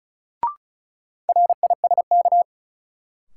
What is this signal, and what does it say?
Morse code, heard as a keyed tone, about 36 wpm, 700 Hz. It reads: RISK
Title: Sentences from Top 600 Words - New Word Drill - 35wpm